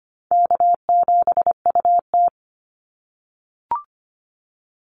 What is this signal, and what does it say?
Morse code, heard as a keyed tone, about 25 wpm, 700 Hz. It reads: K7UT
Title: Easy Call Signs - 25wpm